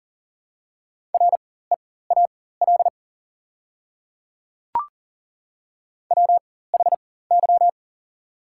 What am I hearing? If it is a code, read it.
Morse code, heard as a keyed tone, about 41 wpm, spaced as for 19 wpm, 700 Hz. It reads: REAL WHY
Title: 4x Character Spacing - Top 300 Words - 40wpm